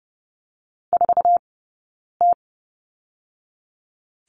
Morse code, heard as a keyed tone, about 30 wpm, 700 Hz. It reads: 4 T
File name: Lesson 20d - Number 3 - Two Characters 30wpm